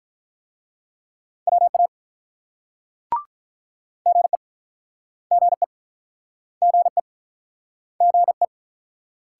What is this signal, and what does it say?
Morse code, heard as a keyed tone, about 48 wpm, 700 Hz. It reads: WA GE GE GE GE